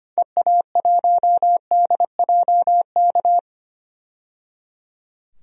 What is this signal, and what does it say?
Morse code, heard as a keyed tone, about 25 wpm, 700 Hz. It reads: EA1DJK
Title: DXCC-100 Call Signs - 6 Characters - 25wpm